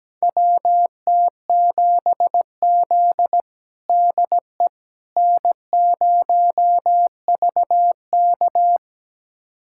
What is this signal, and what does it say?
Morse code, heard as a keyed tone, about 17 wpm, 700 Hz. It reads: WT7Z DE N0VK
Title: USA Callsign DE Callsign - 4 Characters - 17wpm